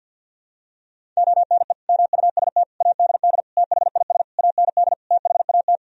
Morse code, heard as a keyed tone, about 50 wpm, 700 Hz. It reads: ONE MUST ADD THIS AND THAT
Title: Sentences from Top 1000 Words - Review Part 1 - 50wpm